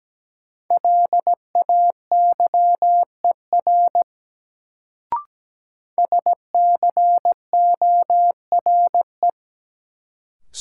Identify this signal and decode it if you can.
Morse code, heard as a keyed tone, about 17 wpm, 700 Hz. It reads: LAYER SCORE
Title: Sentences from Top 1600 Words - New Word Drill - 17wpm